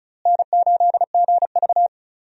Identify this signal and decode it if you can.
Morse code, heard as a keyed tone, about 36 wpm, 700 Hz. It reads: N8GV
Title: USA Call Signs - 4 Characters - Repeated 2x - 35wpm